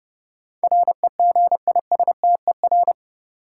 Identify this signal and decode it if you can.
Morse code, heard as a keyed tone, about 31 wpm, 700 Hz. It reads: REGISTER
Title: Sentences from Top 1200 Words - New Word Drill - 30wpm